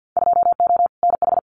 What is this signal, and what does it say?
Morse code, heard as a keyed tone, about 48 wpm, 700 Hz. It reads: 2O NH
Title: Winter Field Day Exchanges 50wpm